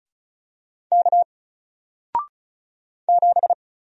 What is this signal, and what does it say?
Morse code, heard as a keyed tone, about 36 wpm, 700 Hz. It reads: K 7
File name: Single-Letter-Number-35wpm